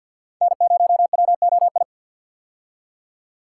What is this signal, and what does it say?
Morse code, heard as a keyed tone, about 53 wpm, 690 Hz. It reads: N0WOI
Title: K1USN SST QSO Practice - 50wpm